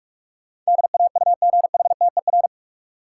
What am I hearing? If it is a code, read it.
Morse code, heard as a keyed tone, about 46 wpm, 700 Hz. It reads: DAUGHTER
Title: Sentences from Top 1500 Words - New Word Drill - 45wpm